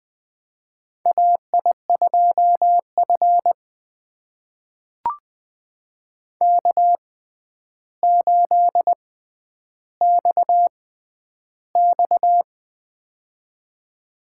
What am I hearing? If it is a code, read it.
Morse code, heard as a keyed tone, about 20 wpm, spaced as for 10 wpm, 700 Hz. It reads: AI2F K8XX
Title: USA Call Signs - 4 Characters - 6x Character Spacing - 20wpm